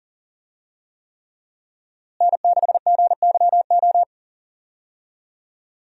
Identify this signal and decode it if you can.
Morse code, heard as a keyed tone, about 40 wpm, 700 Hz. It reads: N6GYO
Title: USA Call Signs - 5 Characters - Repeated 3x - 40wpm